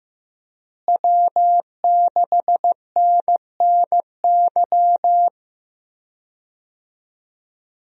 Morse code, heard as a keyed tone, about 15 wpm, 700 Hz. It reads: W6NNY
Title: USA Call Signs - 5 Characters - Repeated 3x - 15wpm